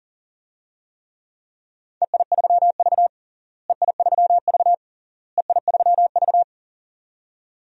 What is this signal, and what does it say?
Morse code, heard as a keyed tone, about 40 wpm, 700 Hz. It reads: EI3V EI3V EI3V
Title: DXCC-100 Call Signs - 4 Characters - Repeated 3x - 40wpm